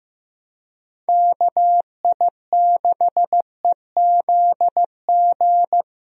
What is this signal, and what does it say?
Morse code, heard as a keyed tone, about 15 wpm, 700 Hz. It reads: KI6EZG
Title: USA Call Signs - 6 Characters - 15wpm